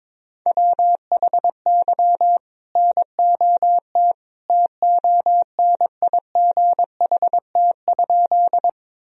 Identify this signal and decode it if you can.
Morse code, heard as a keyed tone, about 22 wpm, 700 Hz. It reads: WHY NOT TONIGHT?
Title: Sentences from Top 1000 Words - Review Part 3 - 22wpm